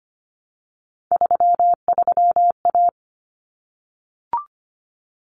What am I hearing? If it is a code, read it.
Morse code, heard as a keyed tone, about 25 wpm, 700 Hz. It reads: 33A
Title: Common Cut Numbers - 3-Digit 25wpm